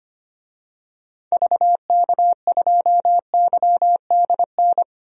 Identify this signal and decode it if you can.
Morse code, heard as a keyed tone, about 25 wpm, 700 Hz. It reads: VK2YDN
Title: DXCC-100 Call Signs - 6 Characters - 25wpm